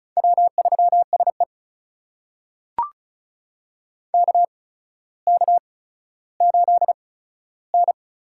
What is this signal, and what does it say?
Morse code, heard as a keyed tone, about 36 wpm, spaced as for 15 wpm, 700 Hz. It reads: W3SE KK8N
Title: USA Call Signs - 4 Characters - 8x Character Spacing - 35wpm